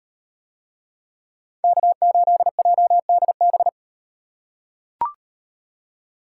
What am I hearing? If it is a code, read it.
Morse code, heard as a keyed tone, about 38 wpm, 700 Hz. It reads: K8JDB